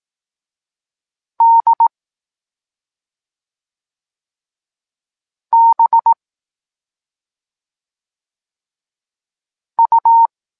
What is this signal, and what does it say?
Morse code, heard as a keyed tone, about 18 wpm, 920 Hz. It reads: D B U
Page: Tekst 14 – doskonalenie odbioru poznanych liter